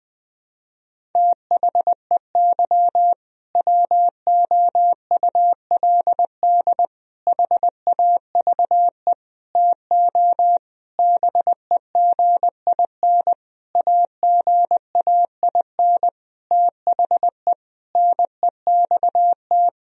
Morse code, heard as a keyed tone, about 21 wpm, 700 Hz. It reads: THEY WOULD HAVE TO BEGIN AGAIN THE NEXT
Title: Sentences from Top 1000 Words - Review Part 1 - 20wpm